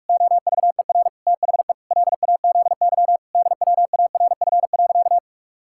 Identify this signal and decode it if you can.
Morse code, heard as a keyed tone, about 43 wpm, 700 Hz. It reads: OVER THE LAZY DWARF.
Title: Pangram Challenge v2 - 45wpm